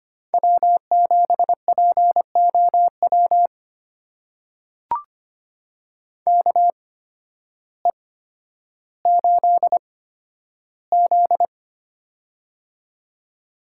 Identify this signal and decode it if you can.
Morse code, heard as a keyed tone, about 25 wpm, spaced as for 12 wpm, 700 Hz. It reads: W7POW KE8Z
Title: USA Call Signs - 8x Character Spacing - 25wpm